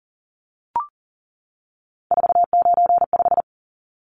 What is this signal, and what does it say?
Morse code, heard as a keyed tone, about 40 wpm, 700 Hz. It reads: 495
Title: Numbers - 3-Digit 40wpm